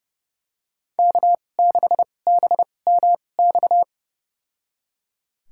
Morse code, heard as a keyed tone, about 30 wpm, spaced as for 22 wpm, 700 Hz. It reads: K6BMX
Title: USA Call Signs - 5 Characters - 2x Character Spacing - 30wpm